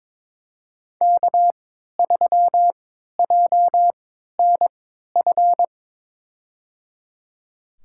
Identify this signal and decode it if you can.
Morse code, heard as a keyed tone, about 22 wpm, spaced as for 12 wpm, 700 Hz. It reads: K3JNF
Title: USA Call Signs - 5 Characters - 3x Character Spacing - 22wpm